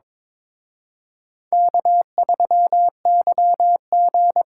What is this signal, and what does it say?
Morse code, heard as a keyed tone, about 26 wpm, 700 Hz. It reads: K3YG
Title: Speed Racing - USA 1x2 Call Signs 22wpm